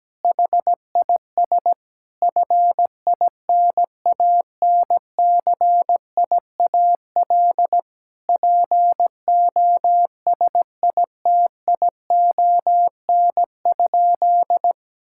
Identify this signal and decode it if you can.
Morse code, heard as a keyed tone, about 17 wpm, 700 Hz. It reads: HIS FINANCIAL POSITION?
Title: Sentences from Top 1000 Words - Review Part 5 - 17wpm